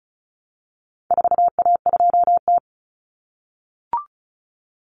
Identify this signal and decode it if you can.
Morse code, heard as a keyed tone, about 34 wpm, 700 Hz. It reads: 4A2T
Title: Common Cut Numbers - 4-Digit 35wpm